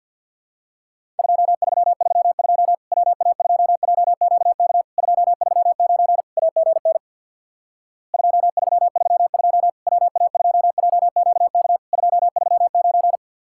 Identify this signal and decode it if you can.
Morse code, heard as a keyed tone, about 50 wpm, 700 Hz. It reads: 2332 WA2JQK 239 AGN 2332 WA2JQK 239
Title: Worked All Europe DX Contest - QTC Exchange with Repeats - 50wpm